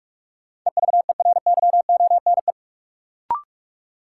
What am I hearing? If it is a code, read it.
Morse code, heard as a keyed tone, about 44 wpm, 700 Hz. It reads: EVERYONE
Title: Sentences from Top 800 Words - New Word Drill - 45wpm